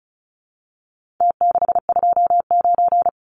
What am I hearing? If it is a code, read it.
Morse code, heard as a keyed tone, about 34 wpm, 700 Hz. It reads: T629
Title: Numbers - Contest Style - 4-Digit 35wpm